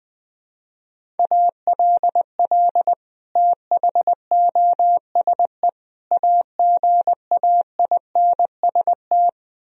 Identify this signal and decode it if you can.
Morse code, heard as a keyed tone, about 20 wpm, 700 Hz. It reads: ALL THOSE AGAINST
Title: Sentences from Top 1000 Words - Review Part 1 - 20wpm